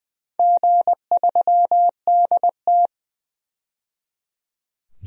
Morse code, heard as a keyed tone, about 20 wpm, 700 Hz. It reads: G3DT
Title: DXCC-100 Call Signs - 4 Characters - 20wpm